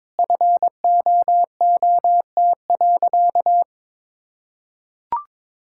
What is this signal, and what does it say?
Morse code, heard as a keyed tone, about 22 wpm, 700 Hz. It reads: FOOT.
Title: Sentences from Top 1000 Words - Review Part 6 - 22wpm